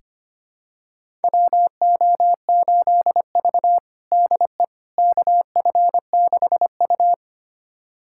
Morse code, heard as a keyed tone, about 25 wpm, 700 Hz. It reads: WO8V DE KF6U
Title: USA Callsign DE Callsign - 4 Characters - 25wpm